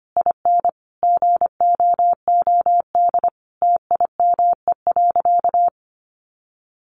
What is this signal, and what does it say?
Morse code, heard as a keyed tone, about 25 wpm, 700 Hz. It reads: IN GOOD TIME.
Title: Sentences from Top 1200 Words 25wpm